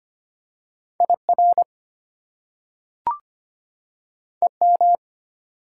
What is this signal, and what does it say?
Morse code, heard as a keyed tone, about 25 wpm, 700 Hz. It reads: IR EM
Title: Prefixes and Suffixes 25wpm